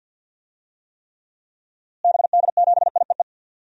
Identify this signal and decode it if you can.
Morse code, heard as a keyed tone, about 47 wpm, 700 Hz. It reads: BD7IEE
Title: DXCC-100 Call Signs - 6 Characters - Repeated 2x - 50wpm